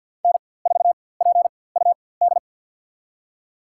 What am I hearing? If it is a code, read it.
Morse code, heard as a keyed tone, about 53 wpm, spaced as for 24 wpm, 700 Hz. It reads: N4PUD
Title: USA Call Signs - 5 Characters - 4x Character Spacing - 50wpm